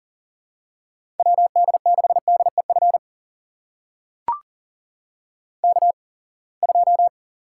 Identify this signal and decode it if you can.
Morse code, heard as a keyed tone, about 40 wpm, spaced as for 28 wpm, 700 Hz. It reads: WD6DEF K2
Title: USA Call Signs - 8x Character Spacing - 40wpm